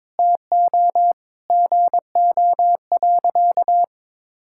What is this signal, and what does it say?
Morse code, heard as a keyed tone, about 22 wpm, 700 Hz. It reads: TO GO.
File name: practice / 22wpm / Sentences with 4-Word Phrases - 22wpm